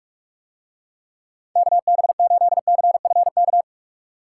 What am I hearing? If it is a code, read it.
Morse code, heard as a keyed tone, about 46 wpm, 700 Hz. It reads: KB8CFK